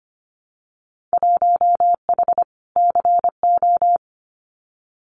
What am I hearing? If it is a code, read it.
Morse code, heard as a keyed tone, about 25 wpm, 700 Hz. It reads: 1H CO
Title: Winter Field Day Exchanges 25wpm